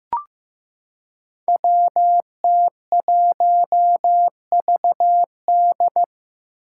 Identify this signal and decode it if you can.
Morse code, heard as a keyed tone, about 15 wpm, 700 Hz. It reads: WT1VD
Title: Five Challenging Letter-Number Combinations 15wpm